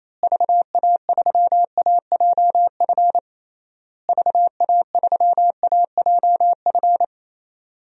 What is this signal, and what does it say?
Morse code, heard as a keyed tone, about 27 wpm, 700 Hz. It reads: VA3AJF VA3AJF
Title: DXCC-100 Call Signs - 6 Characters - Repeated 3x - 28wpm